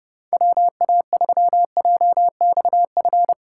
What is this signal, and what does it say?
Morse code, heard as a keyed tone, about 30 wpm, 700 Hz. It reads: WA3JXF
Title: USA Call Signs - 6 Characters - 2x Character Spacing - 30wpm